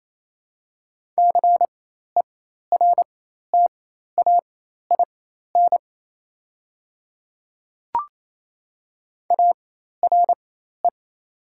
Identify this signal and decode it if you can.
Morse code, heard as a keyed tone, about 28 wpm, spaced as for 13 wpm, 700 Hz. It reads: CERTAIN ARE
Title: 4x Character Spacing - Top 500 Words - 28wpm